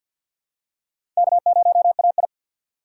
Morse code, heard as a keyed tone, about 46 wpm, 700 Hz. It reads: K0AI
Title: USA Call Signs - 4 Characters - 3x Character Spacing - 50wpm